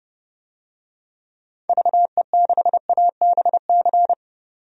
Morse code, heard as a keyed tone, about 30 wpm, 700 Hz. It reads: VE6ABC
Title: DXCC-100 Call Signs - 6 Characters - 30wpm